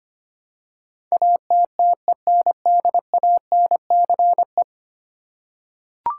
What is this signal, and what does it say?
Morse code, heard as a keyed tone, about 25 wpm, 700 Hz. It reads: ATTENDANCE
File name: 10 Letter Words - First 3 Letters Given - 25wpm